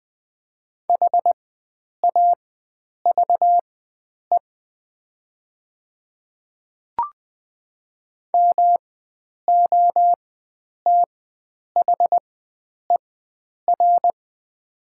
Morse code, heard as a keyed tone, about 20 wpm, spaced as for 9 wpm, 700 Hz. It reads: HAVE MOTHER
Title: 4x Character Spacing - Top 500 Words - 20wpm